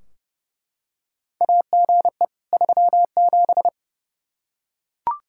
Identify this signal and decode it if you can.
Morse code, heard as a keyed tone, about 30 wpm, 700 Hz. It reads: AGE 37
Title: K1USN SST QSO Practice 30wpm